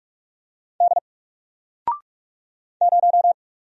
Morse code, heard as a keyed tone, about 46 wpm, 700 Hz. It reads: D 0
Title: Single Letter-Number Alt 45wpm